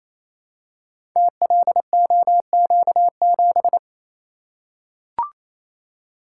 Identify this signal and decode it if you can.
Morse code, heard as a keyed tone, about 29 wpm, 700 Hz. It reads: TLOQ7
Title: Code Groups - 5 Letter:Numbers - 28wpm